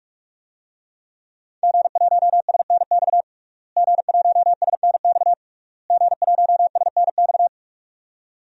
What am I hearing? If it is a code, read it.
Morse code, heard as a keyed tone, about 46 wpm, 700 Hz. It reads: G1SNX G1SNX G1SNX
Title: DXCC-100 Call Signs - 5 Characters - Repeated 3x - 45wpm